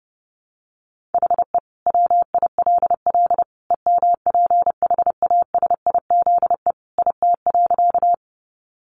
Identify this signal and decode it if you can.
Morse code, heard as a keyed tone, about 30 wpm, 700 Hz. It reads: HE WILL EMPHASIZE IT.
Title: Sentences from Top 1500 Words 30wpm